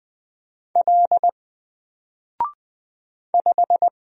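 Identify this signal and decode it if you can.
Morse code, heard as a keyed tone, about 20 wpm, 700 Hz. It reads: L 5
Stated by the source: Single letter and number Morse drill at 20 wpm, from N4LCD.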